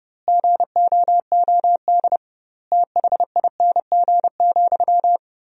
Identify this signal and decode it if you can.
Morse code, heard as a keyed tone, about 30 wpm, 700 Hz. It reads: GOOD THING,
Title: 2x Word-Spacing - Sentences from Top 200 Words - 30wpm